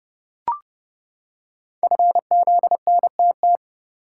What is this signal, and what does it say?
Morse code, heard as a keyed tone, about 30 wpm, 700 Hz. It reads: FZNTT
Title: Code Groups - 5 Letters - 30wpm